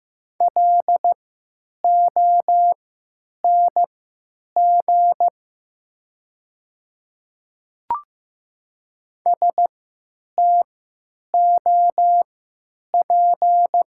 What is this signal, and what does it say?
Morse code, heard as a keyed tone, about 15 wpm, spaced as for 9 wpm, 700 Hz. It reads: LONG STOP
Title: 3x Character Spacing - Top 300 Words - 15wpm